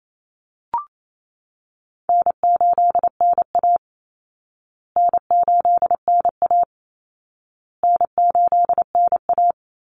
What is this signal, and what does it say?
Morse code, heard as a keyed tone, about 28 wpm, 700 Hz. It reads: N8NA N8NA N8NA
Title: Call Signs Repeated 2x - 28wpm